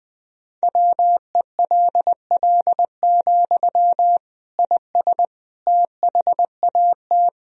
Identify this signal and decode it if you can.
Morse code, heard as a keyed tone, about 20 wpm, 700 Hz. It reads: WELL, IS THAT
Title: Sentences from Top 1000 Words - Review Part 1 - 20wpm